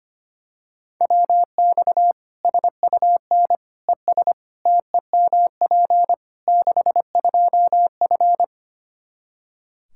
Morse code, heard as a keyed tone, about 25 wpm, 700 Hz. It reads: WX SUN ES TEMP 62F
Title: Practice QSO - Exchange 2 - Info - 25wpm